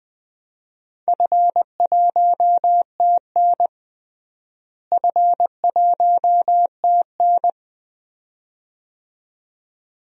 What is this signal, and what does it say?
Morse code, heard as a keyed tone, about 20 wpm, 700 Hz. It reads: F1TN F1TN
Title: DXCC-100 Call Signs - 4 Characters - Repeated 2x - 20wpm